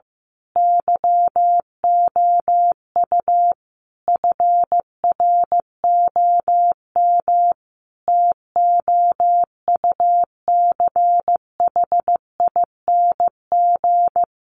Morse code, heard as a keyed tone, about 15 wpm, 700 Hz. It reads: YOU FROM TOUCHING
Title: A Sound Of Thunder 15wpm